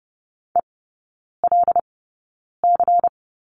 Morse code, heard as a keyed tone, about 30 wpm, spaced as for 9 wpm, 700 Hz. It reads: ELC
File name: Lesson 18e - Letter m - Three Characters 30wpm